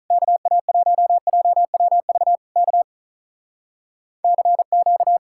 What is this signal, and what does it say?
Morse code, heard as a keyed tone, about 47 wpm, 700 Hz. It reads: KA1JWV K CQ